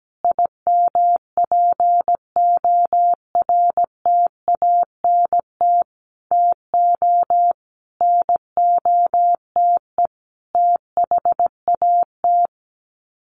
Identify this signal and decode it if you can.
Morse code, heard as a keyed tone, about 17 wpm, 700 Hz. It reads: IMPORTANT TO NOTE THAT
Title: Sets of 5 Words 17wpm